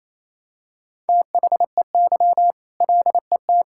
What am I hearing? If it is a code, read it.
Morse code, heard as a keyed tone, about 28 wpm, 700 Hz. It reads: THEY LET
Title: Sentences from Top 1000 Words - Review Part 5 - 28wpm